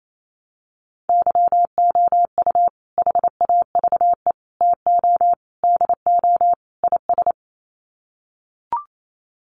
Morse code, heard as a keyed tone, about 28 wpm, 700 Hz. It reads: YOU HAVE TO DO IS
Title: Sets of 5 Words 28wpm